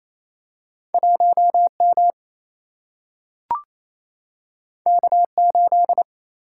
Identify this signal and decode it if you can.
Morse code, heard as a keyed tone, about 28 wpm, 700 Hz. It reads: 1M K8
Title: Single-Letter Single-Number 28wpm